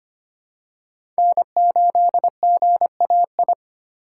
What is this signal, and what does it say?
Morse code, heard as a keyed tone, about 25 wpm, 700 Hz. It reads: N8GAI
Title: USA Call Signs - 2x Character Spacing - 25wpm